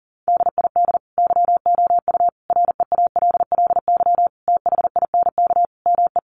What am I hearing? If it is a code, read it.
Morse code, heard as a keyed tone, about 40 wpm, 700 Hz. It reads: DID YOU REALLY THINK ME
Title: Sentences from Top 1200 Words 40wpm